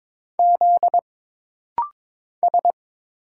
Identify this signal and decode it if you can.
Morse code, heard as a keyed tone, about 22 wpm, 700 Hz. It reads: Z S
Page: This rapid-fire timing is geared to push your ICR (Instant Character Recognition) skills.